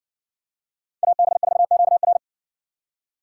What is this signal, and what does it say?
Morse code, heard as a keyed tone, about 58 wpm, 700 Hz. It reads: AB4QR